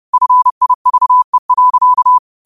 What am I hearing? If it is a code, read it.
Morse code, heard as a keyed tone, about 30 wpm, 1000 Hz. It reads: FIVE.
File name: qotdfiles / 30wpm